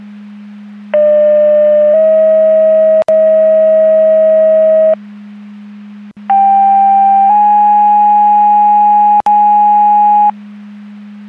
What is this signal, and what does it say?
Two-tone page, heard as a one-second tone, then a three-second tone, about 600 then 640 Hz, about 790 then 830 Hz. It is heard from 0.9 to 10.3 s.
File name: Portage / 00001 08-24 Portage 154.385 Paging